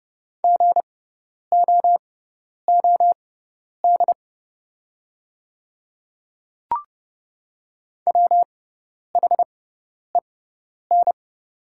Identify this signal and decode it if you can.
Morse code, heard as a keyed tone, about 30 wpm, spaced as for 10 wpm, 700 Hz. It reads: GOOD WHEN
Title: 6x Character Spacing - Top 200 Words - 30wpm